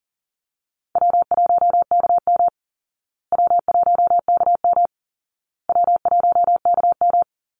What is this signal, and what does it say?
Morse code, heard as a keyed tone, about 39 wpm, 700 Hz. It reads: W1KM W1KM W1KM
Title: Easy Call Signs Repeated 3x - 40wpm